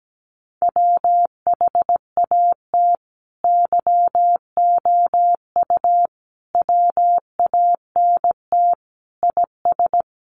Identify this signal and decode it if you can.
Morse code, heard as a keyed tone, about 17 wpm, 700 Hz. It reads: WHAT YOU WANT IS
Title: Sentences from Top 700 Words 17wpm